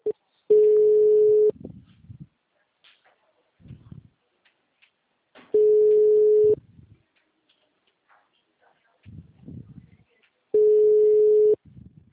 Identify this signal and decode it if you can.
Ringback tone (Europe, 425 Hz), with speech.
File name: out- -2003- - - .10342